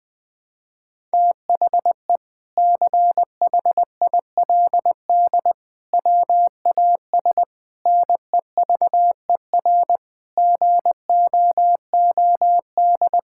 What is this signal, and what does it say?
Morse code, heard as a keyed tone, about 20 wpm, 700 Hz. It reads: THE CHILD WAS NEVER GOOD
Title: Sentences from Top 1000 Words - Review Part 6 - 20wpm